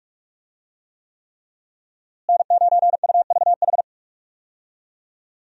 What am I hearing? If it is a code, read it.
Morse code, heard as a keyed tone, about 46 wpm, 700 Hz. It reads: N9UVH
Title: USA Call Signs - 5 Characters - Repeated 3x - 45wpm